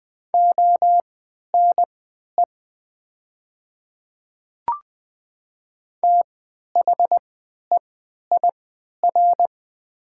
Morse code, heard as a keyed tone, about 20 wpm, spaced as for 11 wpm, 700 Hz. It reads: ONE THEIR
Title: 3x Character Spacing - Top 100 Words - 20wpm